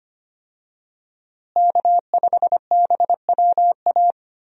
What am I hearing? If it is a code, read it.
Morse code, heard as a keyed tone, about 25 wpm, 700 Hz. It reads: K5BWA
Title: USA Call Signs - 5 Characters - Repeated 3x - 25wpm